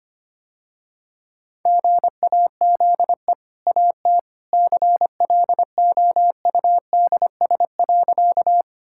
Morse code, heard as a keyed tone, about 25 wpm, 700 Hz. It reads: GAZE AT CLOUDS.
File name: Easy Sentences with Hints - 25wpm